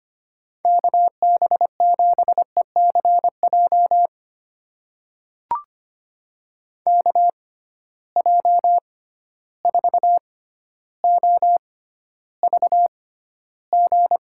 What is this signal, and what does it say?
Morse code, heard as a keyed tone, about 25 wpm, spaced as for 13 wpm, 700 Hz. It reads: KB7ECJ KJ4OVG
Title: USA Call Signs - 6 Characters - 6x Character Spacing - 25wpm